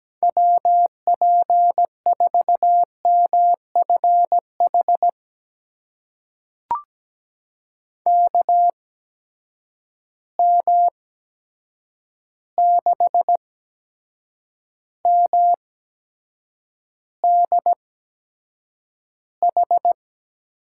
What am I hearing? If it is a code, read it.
Morse code, heard as a keyed tone, about 17 wpm, spaced as for 7 wpm, 700 Hz. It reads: WP4MFH KM6MDH
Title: USA Call Signs - 6 Characters - 8x Character Spacing - 17wpm